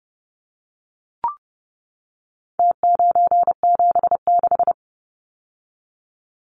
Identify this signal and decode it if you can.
Morse code, heard as a keyed tone, about 30 wpm, 700 Hz. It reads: T976
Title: Numbers - Contest Style - 4-Digit 30wpm